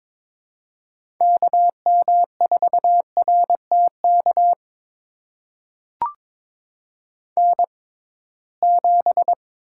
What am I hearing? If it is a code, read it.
Morse code, heard as a keyed tone, about 21 wpm, spaced as for 16 wpm, 700 Hz. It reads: KM4RTK N7
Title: USA Call Signs - 6x Character Spacing - 22wpm